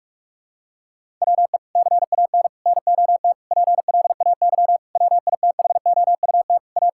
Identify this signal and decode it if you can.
Morse code, heard as a keyed tone, about 46 wpm, 700 Hz. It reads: WE CAN NOT PLAY WITHOUT A
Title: Sentences from Top 1000 Words - Review Part 6 - 45wpm